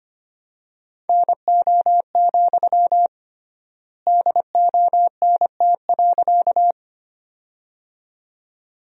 Morse code, heard as a keyed tone, about 25 wpm, 700 Hz. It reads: NO, DONT.
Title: 3x Word-Spacing - Sentences from Top 300 Words - 25wpm